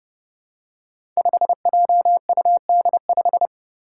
Morse code, heard as a keyed tone, about 31 wpm, 700 Hz. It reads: 5JUD5
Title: Five Challenging Letter-Number Combinations 30wpm